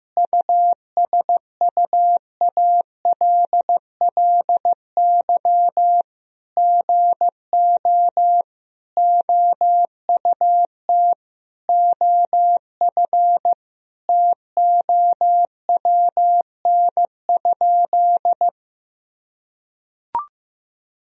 Morse code, heard as a keyed tone, about 14 wpm, 700 Hz. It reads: USUALLY GO OUT OF TOWN?
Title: Sentences from Top 1000 Words - Review Part 3 - 15wpm